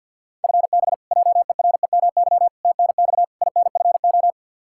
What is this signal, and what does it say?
Morse code, heard as a keyed tone, about 52 wpm, 700 Hz. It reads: FB JEREMY TNX INFO
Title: Practice QSO - Exchange 3 - More Info - 50wpm